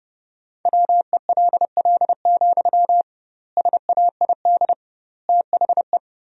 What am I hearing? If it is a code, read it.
Morse code, heard as a keyed tone, about 32 wpm, 700 Hz. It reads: WELL, SAID THE
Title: 2x Word-Spacing - Sentences from Top 400 Words - 30wpm